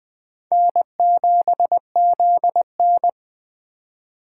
Morse code, heard as a keyed tone, about 20 wpm, 700 Hz. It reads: N7ZN
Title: K1USN SST QSO Practice - 20wpm